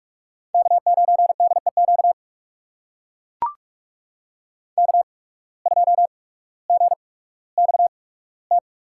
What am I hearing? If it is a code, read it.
Morse code, heard as a keyed tone, about 44 wpm, spaced as for 19 wpm, 700 Hz. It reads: K9DEQ K2GXT
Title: USA Call Signs - 5 Characters - 8x Character Spacing - 45wpm